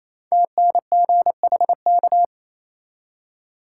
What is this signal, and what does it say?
Morse code, heard as a keyed tone, about 27 wpm, 700 Hz. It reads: TNGHK
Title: Code Groups - 5 Letter:Numbers - 28wpm